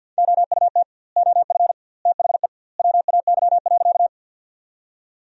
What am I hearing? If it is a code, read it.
Morse code, heard as a keyed tone, about 46 wpm, 700 Hz. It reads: OUT OF THE WAY.
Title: 2x Word-Spacing - Sentences from Top 100 Words - 50wpm